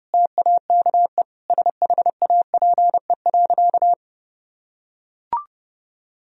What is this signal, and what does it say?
Morse code, heard as a keyed tone, about 30 wpm, 700 Hz. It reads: TAKE SHAPE.
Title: Sentences from Top 1000 Words - Review Part 3 - 30wpm